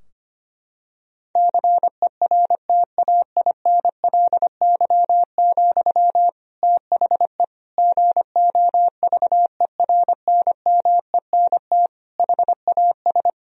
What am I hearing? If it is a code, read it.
Morse code, heard as a keyed tone, about 25 wpm, 700 Hz. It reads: CERTAINLY, THE GOVERNMENT HAS
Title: Sentences from Top 1000 Words - Review Part 3 - 25wpm